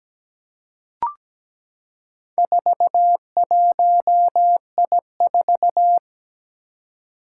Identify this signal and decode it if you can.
Morse code, heard as a keyed tone, about 17 wpm, 700 Hz. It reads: 41I4
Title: Four Challenging Letter-Number Combinations 17wpm